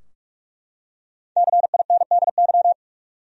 Morse code, heard as a keyed tone, about 44 wpm, 700 Hz. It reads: CINDY
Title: USA Common Names - Max 5 Letters - 45wpm